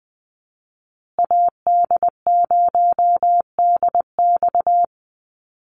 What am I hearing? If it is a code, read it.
Morse code, heard as a keyed tone, about 20 wpm, 700 Hz. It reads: AD0DX
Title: Call Signs Repeated 3x - 20wpm